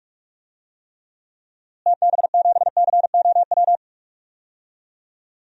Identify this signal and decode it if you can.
Morse code, heard as a keyed tone, about 46 wpm, 700 Hz. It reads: TB7COW
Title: DXCC-100 Call Signs - 6 Characters - Repeated 2x - 45wpm